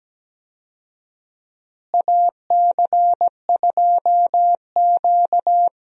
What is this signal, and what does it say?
Morse code, heard as a keyed tone, about 17 wpm, 700 Hz. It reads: AC2Q
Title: USA Call Signs - 4 Characters - Repeated 2x - 17wpm